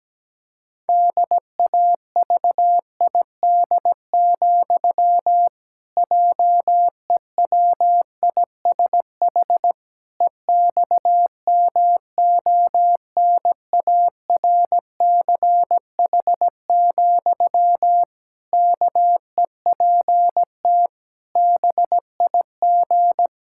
Morse code, heard as a keyed tone, about 17 wpm, 700 Hz. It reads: DAVID, JEWISH EXMONARCH, KEPT BIG
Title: Pangram Challenge v2 - 17wpm